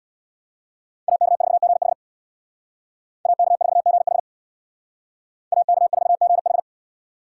Morse code, heard as a keyed tone, about 63 wpm, 700 Hz. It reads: AK4GH AK4GH AK4GH